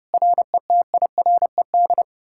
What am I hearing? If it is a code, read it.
Morse code, heard as a keyed tone, about 37 wpm, 700 Hz. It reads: RETIRED
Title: Speed-Racing Common QSO Phrases Encore - 30wpm